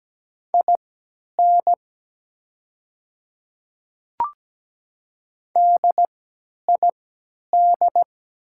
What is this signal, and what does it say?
Morse code, heard as a keyed tone, about 17 wpm, spaced as for 10 wpm, 700 Hz. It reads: IN DID
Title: 3x Character Spacing - Top 100 Words - 17wpm